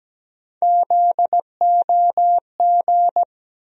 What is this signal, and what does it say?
Morse code, heard as a keyed tone, about 17 wpm, 700 Hz. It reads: ZOG
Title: USA Call Sign Postfixes - 3 Letters 17wpm